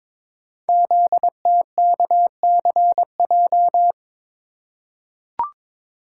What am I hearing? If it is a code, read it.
Morse code, heard as a keyed tone, about 22 wpm, 700 Hz. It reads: ZTKCJ
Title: Code Groups - 5 Letters - 22wpm